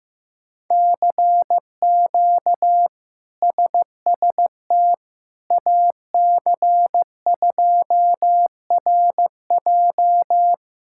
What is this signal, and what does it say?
Morse code, heard as a keyed tone, about 15 wpm, 700 Hz. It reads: CQ SST AC2RJ
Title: K1USN SST QSO Practice - 15wpm